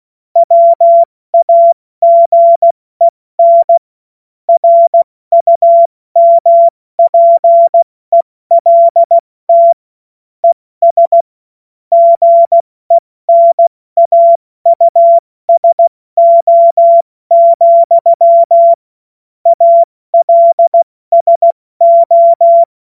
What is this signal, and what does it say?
Morse code, heard as a keyed tone, about 16 wpm, 680 Hz. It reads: WAGEN RUMPELT ES GENAUSO, ALSO
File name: Morsekurs / Witze / Witz 21. Juni 2022 70 F80